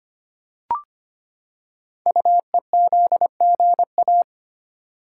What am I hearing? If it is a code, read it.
Morse code, heard as a keyed tone, about 25 wpm, 700 Hz. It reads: UEZGA
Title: Code Groups - 5 Letter:Numbers - 25wpm